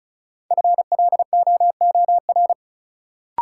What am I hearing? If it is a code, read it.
Morse code, heard as a keyed tone, about 36 wpm, 700 Hz. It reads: FLOOR
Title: Sets of 3 Words - New Word Drill - 35wpm